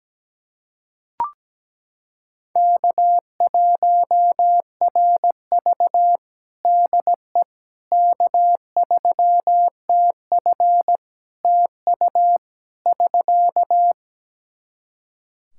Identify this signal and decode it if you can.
Morse code, heard as a keyed tone, about 17 wpm, 700 Hz. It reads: K1RV DE K3TF TU <SK>
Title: Practice QSO - Exchange 4 - Ending - 17wpm